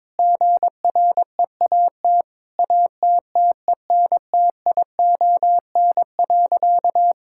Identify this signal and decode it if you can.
Morse code, heard as a keyed tone, about 22 wpm, 700 Hz. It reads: GREAT ATTENTION.
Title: Sentences from Top 1000 Words - Review Part 6 - 22wpm